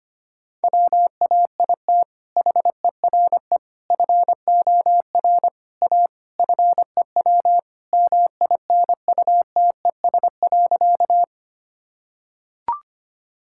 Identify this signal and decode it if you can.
Morse code, heard as a keyed tone, about 25 wpm, 700 Hz. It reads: WAIT HERE FOR A FEW MINUTES.
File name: Sentences with 3-Word Phrases - 25wpm